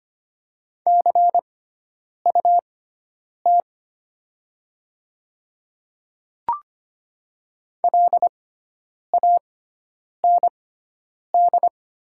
Morse code, heard as a keyed tone, about 25 wpm, spaced as for 9 wpm, 700 Hz. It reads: CUT LAND
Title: 6x Character Spacing - Top 300 Words - 25wpm